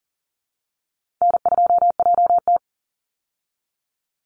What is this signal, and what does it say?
Morse code, heard as a keyed tone, about 40 wpm, 700 Hz. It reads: N2JT
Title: Easy Call Signs Repeated 3x - 40wpm